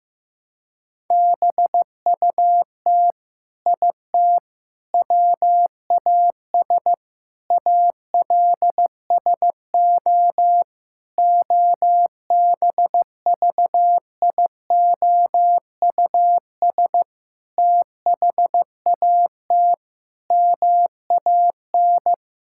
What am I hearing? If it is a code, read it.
Morse code, heard as a keyed tone, about 15 wpm, 700 Hz. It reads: BUT IT WAS ALSO OBVIOUS THAT MAN
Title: Sentences from Top 1600 Words 15wpm